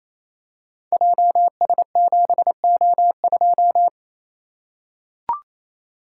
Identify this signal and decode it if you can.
Morse code, heard as a keyed tone, about 29 wpm, 700 Hz. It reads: JS7O2
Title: Code Groups - 5 Letter:Numbers - 28wpm